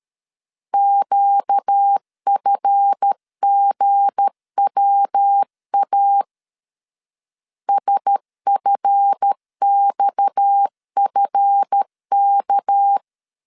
Morse code, heard as a keyed tone, about 13 wpm, 800 Hz. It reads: QFGWA SFXFK
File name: LEZIONI / 7 GF9Z / 7 GF 12wpm 2122